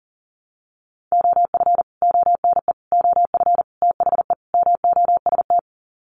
Morse code, heard as a keyed tone, about 40 wpm, 700 Hz. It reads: OF ONE OF THE MOST
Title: Sets of 5 Words Encore2 40wpm